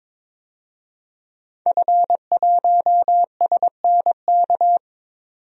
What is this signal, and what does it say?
Morse code, heard as a keyed tone, about 22 wpm, 700 Hz. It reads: F1SNK
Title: DXCC-100 Call Signs - 5 Characters - 22wpm